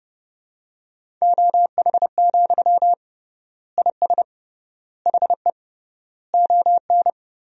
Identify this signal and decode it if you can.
Morse code, heard as a keyed tone, about 29 wpm, 700 Hz. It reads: OH, IS HE ON
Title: 3x Word-Spacing - Sentences from Top 500 Words - 30wpm